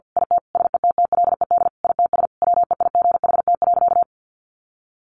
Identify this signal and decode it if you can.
Morse code, heard as a keyed tone, about 50 wpm, 700 Hz. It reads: IT SETTLED ITS WEIGHT.
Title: A Sound Of Thunder 50wpm